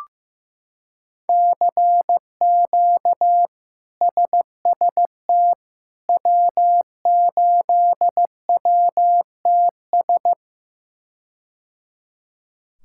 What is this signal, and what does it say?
Morse code, heard as a keyed tone, about 16 wpm, 700 Hz. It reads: CQ SST W8WTS
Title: K1USN SST QSO Practice - 15wpm